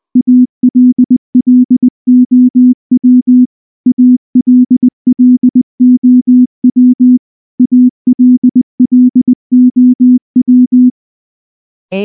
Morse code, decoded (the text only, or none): ALLOW ALLOW ALLOW